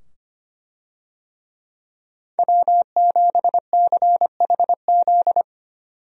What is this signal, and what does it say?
Morse code, heard as a keyed tone, about 25 wpm, 700 Hz. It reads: W7CHZ
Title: USA Call Signs - 5 Characters - 25wpm